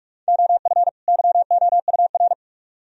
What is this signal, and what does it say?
Morse code, heard as a keyed tone, about 55 wpm, 700 Hz. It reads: OF YOUR
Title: Speed-Racing - Sets of 2 Words 45wpm